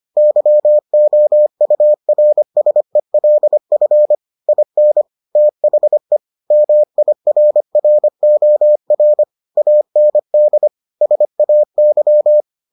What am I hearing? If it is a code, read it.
Morse code, heard as a keyed tone, about 25 wpm, 600 Hz. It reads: YOURSELF IN THE MIRROR AND SAY